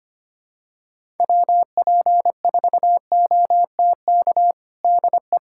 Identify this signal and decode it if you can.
Morse code, heard as a keyed tone, about 25 wpm, 700 Hz. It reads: WP4OTK DE
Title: USA Callsign DE Callsign - 6 Characters - 25wpm